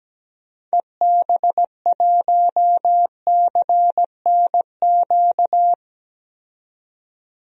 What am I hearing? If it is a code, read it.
Morse code, heard as a keyed tone, about 17 wpm, 700 Hz. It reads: EB1CNQ
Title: DXCC-100 Call Signs - 6 Characters - 17wpm